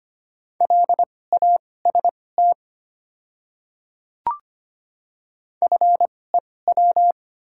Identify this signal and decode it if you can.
Morse code, heard as a keyed tone, about 25 wpm, spaced as for 18 wpm, 700 Hz. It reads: LAST FEW
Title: 2x Character Spacing - Top 300 Words - 25wpm